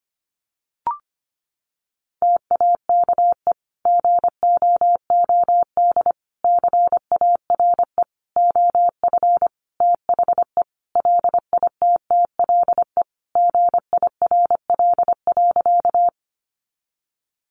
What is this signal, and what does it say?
Morse code, heard as a keyed tone, about 25 wpm, 700 Hz. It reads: TAKE GOOD CARE OF THE LITTLE GIRL.
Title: Sentences from Top 400 Words 25wpm